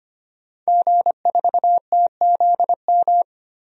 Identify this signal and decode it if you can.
Morse code, heard as a keyed tone, about 25 wpm, 700 Hz. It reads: G4TZM
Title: DXCC-100 Call Signs - 5 Characters - Repeated 2x - 25wpm